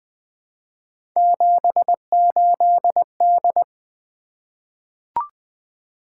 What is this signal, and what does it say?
Morse code, heard as a keyed tone, about 20 wpm, 700 Hz. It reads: 78D
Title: Three Letter-Number Combinations - 20wpm